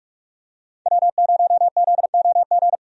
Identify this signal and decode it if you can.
Morse code, heard as a keyed tone, about 42 wpm, 700 Hz. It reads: W0ZOG
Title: USA Call Signs - 5 Characters - 45wpm